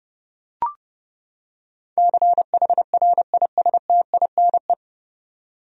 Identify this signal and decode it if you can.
Morse code, heard as a keyed tone, about 31 wpm, 700 Hz. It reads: CHRISTINE
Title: USA Common Names - 30wpm